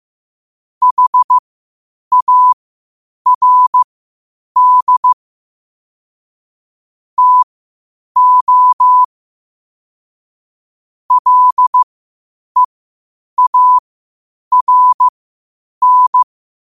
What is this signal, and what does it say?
Morse code, heard as a keyed tone, about 15 wpm, spaced as for 8 wpm, 1000 Hz. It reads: HARD TO LEARN
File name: qotdfiles / 8wpm